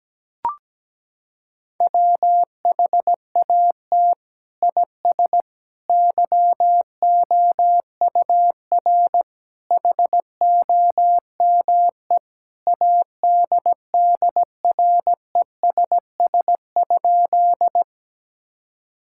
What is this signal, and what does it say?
Morse code, heard as a keyed tone, about 17 wpm, 700 Hz. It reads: WHAT IS YOUR HOME ADDRESS?
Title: Sentences from Top 1000 Words - Review Part 6 - 17wpm